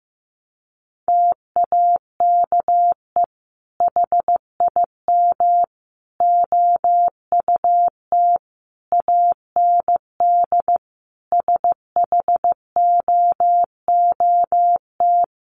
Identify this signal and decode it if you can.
Morse code, heard as a keyed tone, about 14 wpm, 700 Hz. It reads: TAKE HIM OUT AND SHOOT
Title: Sentences from Top 1200 Words 15wpm